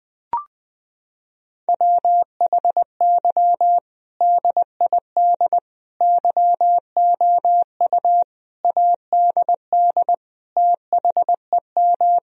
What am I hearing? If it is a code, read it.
Morse code, heard as a keyed tone, about 19 wpm, 700 Hz. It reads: WHY DID YOU ADD THEM
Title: Sentences from Top 1000 Words - Review Part 1 - 20wpm